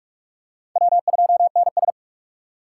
Morse code, heard as a keyed tone, about 45 wpm, 700 Hz. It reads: W2NS
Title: USA Call Signs - 4 Characters - 4x Character Spacing - 45wpm